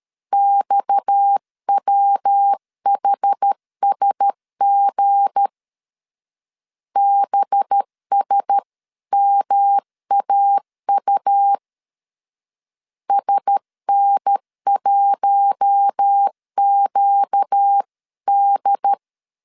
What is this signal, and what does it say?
Morse code, heard as a keyed tone, about 13 wpm, 800 Hz. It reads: XWHSG BSMAU SN1QD
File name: LEZIONI / 7 GF9Z / 7 GF 12wpm 2122